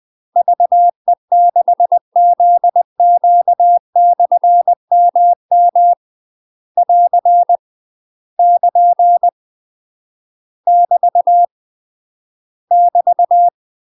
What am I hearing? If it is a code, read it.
Morse code, heard as a keyed tone, about 20 wpm, 700 Hz. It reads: VE6ZQ/MM <AR> <KN> <BT> <BT>
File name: UK MASTER QSOs TRAINING / 20WPM / UK MASTER QSOs 0037